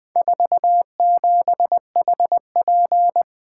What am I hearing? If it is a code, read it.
Morse code, heard as a keyed tone, about 20 wpm, 700 Hz. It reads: 47HP
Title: Four Letter-Number Combinations - 20wpm